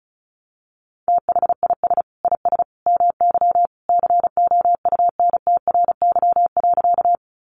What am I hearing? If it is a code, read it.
Morse code, heard as a keyed tone, about 34 wpm, 700 Hz. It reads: THIS IS MY COUNTRY.
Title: Sentences from Top 300 Words 35wpm